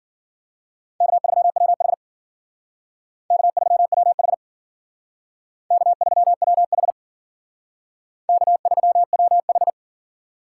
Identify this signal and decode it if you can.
Morse code, heard as a keyed tone, about 51 wpm, 700 Hz. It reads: K3WH K3WH K3WH K3WH